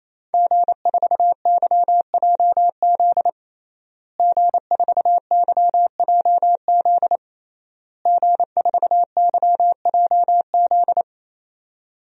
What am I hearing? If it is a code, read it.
Morse code, heard as a keyed tone, about 28 wpm, 700 Hz. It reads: G4YJZ G4YJZ G4YJZ
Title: DXCC-100 Call Signs - 5 Characters - Repeated 3x - 28wpm